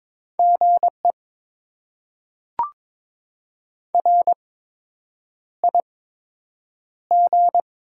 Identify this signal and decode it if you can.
Morse code, heard as a keyed tone, about 22 wpm, spaced as for 8 wpm, 700 Hz. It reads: GE RIG
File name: Top 20 QSO Elements - 8x Character Spacing - 22wpm